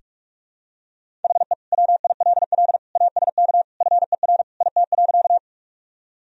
Morse code, heard as a keyed tone, about 41 wpm, 700 Hz. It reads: HE WILL ASK FER IT.
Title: Sentences from Top 200 Words and Abbreviations 45wpm